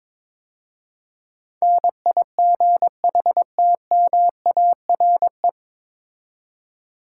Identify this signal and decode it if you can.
Morse code, heard as a keyed tone, about 22 wpm, 700 Hz. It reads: NIGHTMARE
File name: 9 Letter Words - First 3 Letters Given - 22wpm